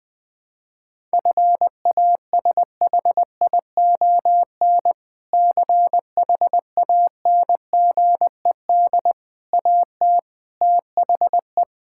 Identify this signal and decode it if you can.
Morse code, heard as a keyed tone, about 20 wpm, 700 Hz. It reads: FASHION CHANGED AT THE
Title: Sentences with 5-Word Phrases - Spoken Beginning - 20wpm